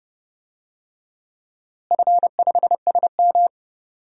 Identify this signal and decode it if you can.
Morse code, heard as a keyed tone, about 30 wpm, 700 Hz. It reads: F5SM
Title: DXCC-100 Call Signs - 4 Characters - Repeated 2x - 30wpm